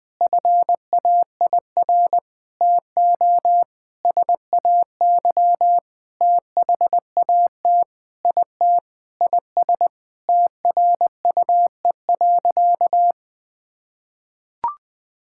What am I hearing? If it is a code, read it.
Morse code, heard as a keyed tone, about 20 wpm, 700 Hz. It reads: FAIR TO SAY THAT IT IS TRUE.
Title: Sentences with 8-Word Phrases - 20wpm